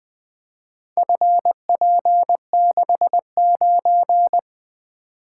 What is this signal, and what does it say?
Morse code, heard as a keyed tone, about 20 wpm, 700 Hz. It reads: FP69
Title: Four Letter-Number Combinations - 20wpm